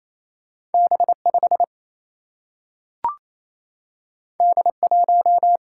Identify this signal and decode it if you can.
Morse code, heard as a keyed tone, about 28 wpm, 700 Hz. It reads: B5 D1
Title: Single-Letter Single-Number 28wpm